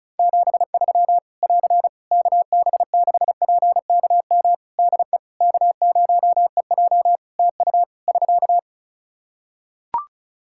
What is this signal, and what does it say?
Morse code, heard as a keyed tone, about 36 wpm, 700 Hz. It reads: 73 <AR> KB6PKM DE K0EJ TU <SK>
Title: Practice QSO - Exchange 4 - Ending - 35wpm